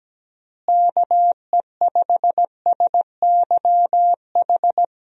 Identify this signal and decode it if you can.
Morse code, heard as a keyed tone, about 17 wpm, 700 Hz. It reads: KE5SYH
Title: USA Call Signs - 6 Characters - Repeated 2x - 17wpm